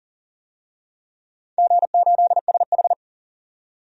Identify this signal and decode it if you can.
Morse code, heard as a keyed tone, about 40 wpm, 700 Hz. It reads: G8SH
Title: DXCC-100 Call Signs - 4 Characters - 40wpm